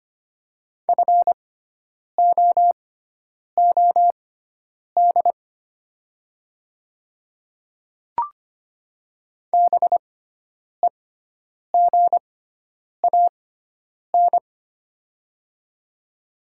Morse code, heard as a keyed tone, about 25 wpm, spaced as for 9 wpm, 700 Hz. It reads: FOOD BEGAN
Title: 6x Character Spacing - Top 300 Words - 25wpm